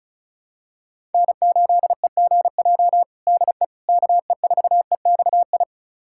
Morse code, heard as a keyed tone, about 34 wpm, 700 Hz. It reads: N8EGJ DE KE4EXI
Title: USA Callsign DE Callsign 35wpm